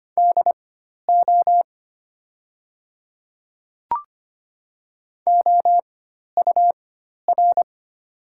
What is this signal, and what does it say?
Morse code, heard as a keyed tone, about 25 wpm, spaced as for 12 wpm, 700 Hz. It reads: DO OUR
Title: 4x Character Spacing - Top 300 Words - 25wpm